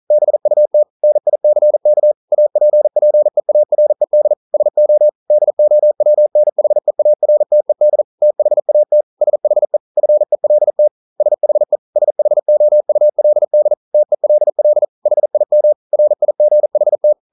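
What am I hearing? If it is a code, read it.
Morse code, heard as a keyed tone, about 40 wpm, 600 Hz. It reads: BUT NICK APPEARED SO DOWNHEARTED THAT SHE FELT SHE SHOULD TELL HIM RIGHT
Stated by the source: Morse code practice at 41 wpm, from CW med Gnister.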